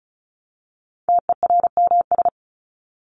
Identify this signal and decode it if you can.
Morse code, heard as a keyed tone, about 34 wpm, 700 Hz. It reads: TERMS
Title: ES:S Suffix - Top 100 Words with Base Words 35wpm